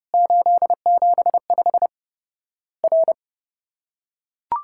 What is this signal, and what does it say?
Morse code, heard as a keyed tone, about 30 wpm, 700 Hz. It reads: 875 R
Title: Worked All Europe DX Contest - QTC Exchange - 30wpm